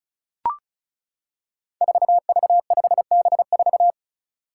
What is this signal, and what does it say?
Morse code, heard as a keyed tone, about 36 wpm, 700 Hz. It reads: 4V5B4
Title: Five Challenging Letter-Number Combinations 35wpm